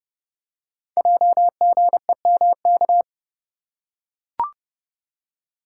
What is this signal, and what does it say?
Morse code, heard as a keyed tone, about 30 wpm, 700 Hz. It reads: JGEMK
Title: Code Groups - 5 Letters - 30wpm